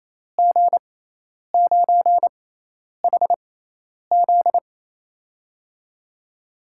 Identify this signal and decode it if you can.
Morse code, heard as a keyed tone, about 28 wpm, spaced as for 10 wpm, 700 Hz. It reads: G9HZ
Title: USA Call Signs - 6x Character Spacing - 28wpm